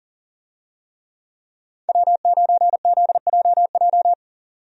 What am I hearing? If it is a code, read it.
Morse code, heard as a keyed tone, about 40 wpm, 700 Hz. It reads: W9ZJJ
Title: USA Call Signs - 5 Characters - 40wpm